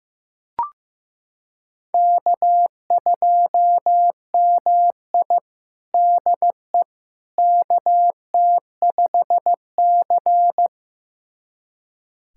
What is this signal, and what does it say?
Morse code, heard as a keyed tone, about 16 wpm, 700 Hz. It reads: K2MI DE KT5C
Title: USA Callsign DE Callsign - 4 Characters - 15wpm